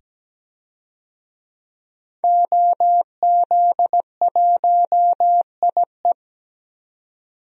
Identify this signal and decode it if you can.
Morse code, heard as a keyed tone, about 17 wpm, 700 Hz. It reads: OZ1IE
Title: DXCC-100 Call Signs - 5 Characters - 17wpm